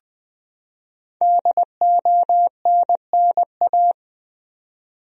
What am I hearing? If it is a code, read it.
Morse code, heard as a keyed tone, about 20 wpm, 700 Hz. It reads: DONNA
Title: USA Common Names - 20wpm